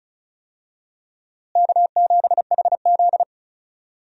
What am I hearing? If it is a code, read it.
Morse code, heard as a keyed tone, about 34 wpm, 700 Hz. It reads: K7HZ
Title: USA Call Signs - 4 Characters - Repeated 3x - 35wpm